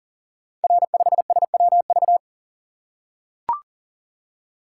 Morse code, heard as a keyed tone, about 39 wpm, 700 Hz. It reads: R5SWV
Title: Code Groups - 5 Letter:Numbers - 40wpm